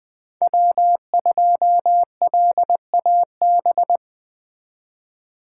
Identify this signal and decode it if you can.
Morse code, heard as a keyed tone, about 20 wpm, 700 Hz. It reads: W2LAB